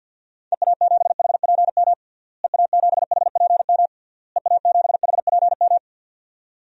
Morse code, heard as a keyed tone, about 51 wpm, 700 Hz. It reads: EA7HPM EA7HPM EA7HPM
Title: DXCC-100 Call Signs - 6 Characters - Repeated 3x - 50wpm